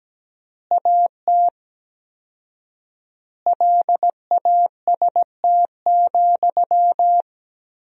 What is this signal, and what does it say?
Morse code, heard as a keyed tone, about 17 wpm, 700 Hz. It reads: AT LAST,
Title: 4x Word-Spacing - Sentences from Top 300 Words - 17wpm